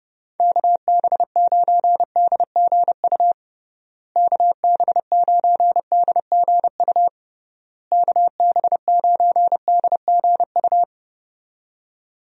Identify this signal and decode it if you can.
Morse code, heard as a keyed tone, about 30 wpm, 700 Hz. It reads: KB9DGU KB9DGU KB9DGU
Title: USA Call Signs - 6 Characters - Repeated 3x - 30wpm